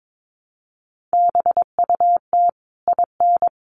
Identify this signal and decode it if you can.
Morse code, heard as a keyed tone, about 22 wpm, 700 Hz. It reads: BUT IN
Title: Sentences from Top 1500 Words 22wpm